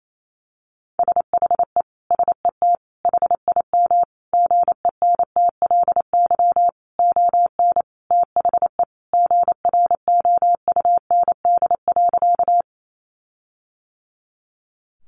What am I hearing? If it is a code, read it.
Morse code, heard as a keyed tone, about 28 wpm, 700 Hz. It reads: SHE SET HIM GENTLY ON THE GROUND.
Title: Sentences from Top 1500 Words 28wpm